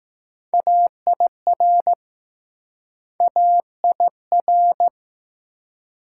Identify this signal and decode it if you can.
Morse code, heard as a keyed tone, about 21 wpm, 700 Hz. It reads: AIR AIR